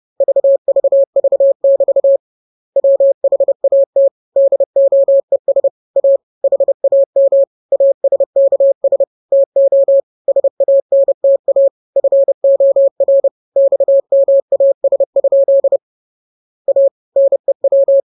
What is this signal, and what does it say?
Morse code, heard as a keyed tone, about 30 wpm, 550 Hz. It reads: VVV<BT> WHAT DOES A HAM ASKS TO SANTA FOR XMAS? A NEW